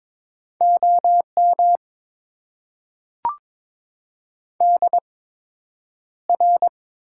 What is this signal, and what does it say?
Morse code, heard as a keyed tone, about 22 wpm, spaced as for 9 wpm, 700 Hz. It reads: OM DR
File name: Top 90 QSO Elements - 8x Character Spacing - 22wpm